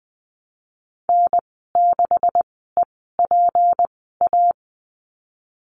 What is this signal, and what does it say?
Morse code, heard as a keyed tone, about 20 wpm, spaced as for 15 wpm, 700 Hz. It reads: N6EPA
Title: 167 Lesson 27g - Number 7 - Call Signs 20wpm